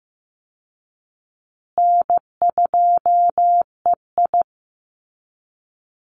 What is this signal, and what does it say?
Morse code, heard as a keyed tone, about 15 wpm, 700 Hz. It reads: N2EI
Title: Call Signs Repeated 3x - 15wpm